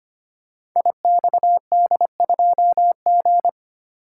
Morse code, heard as a keyed tone, about 25 wpm, 700 Hz. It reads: IXD2G
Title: Code Groups - 5 Letter:Numbers - 25wpm